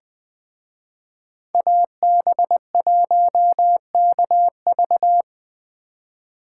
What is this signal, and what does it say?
Morse code, heard as a keyed tone, about 20 wpm, 700 Hz. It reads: AB1KV
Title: Speed Racing - USA Call Signs 17wpm